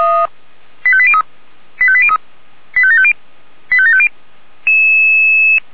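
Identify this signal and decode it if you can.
Five-tone selective call (ZVEI), heard as quick runs of five tones, tones of about 70 ms. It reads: 85803 85803 85750 85750
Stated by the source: ZVEI five-tone selective call on 464 received by SDR.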